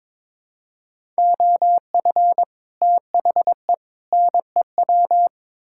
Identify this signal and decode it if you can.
Morse code, heard as a keyed tone, about 22 wpm, 700 Hz. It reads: OF THE NEW
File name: practice / 22wpm / Build A Phrase - 5-Word Phrases - 22wpm